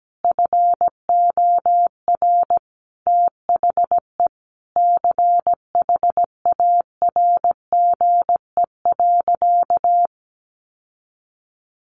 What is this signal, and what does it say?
Morse code, heard as a keyed tone, about 17 wpm, 700 Hz. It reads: FOR THE CHARGE.
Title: Sentences from Top 1300 Words 17wpm